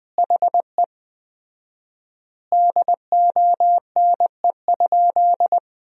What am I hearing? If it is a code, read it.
Morse code, heard as a keyed tone, about 20 wpm, 700 Hz. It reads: HE DONE?
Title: 4x Word-Spacing - Sentences from Top 500 Words - 20wpm